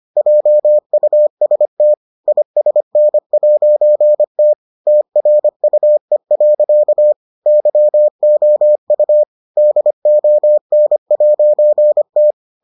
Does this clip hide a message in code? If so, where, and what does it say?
Yes: Morse code from 0.2 to 12.3 s, heard as a keyed tone, about 25 wpm, 600 Hz. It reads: JUST ISN'T TRUE. YOU DON'T